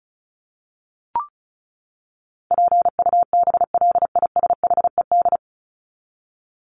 Morse code, heard as a keyed tone, about 34 wpm, 700 Hz. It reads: PUBLISHED
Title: ED Suffix - Top 100 Words with Base Words 35wpm